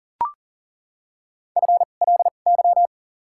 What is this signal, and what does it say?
Morse code, heard as a keyed tone, about 40 wpm, spaced as for 29 wpm, 700 Hz. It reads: FLY
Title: 2x Character Spacing - Top 500 Words - 40wpm